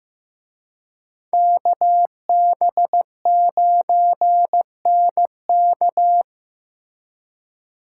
Morse code, heard as a keyed tone, about 18 wpm, 700 Hz. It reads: KB9NK
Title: Speed Racing - USA 2x2 Call Signs 15wpm